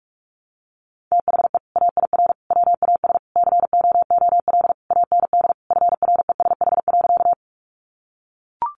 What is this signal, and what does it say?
Morse code, heard as a keyed tone, about 44 wpm, 700 Hz. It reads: THE AIR WAS COOL AND FRESH.
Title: Sentences from Top 1200 Words 45wpm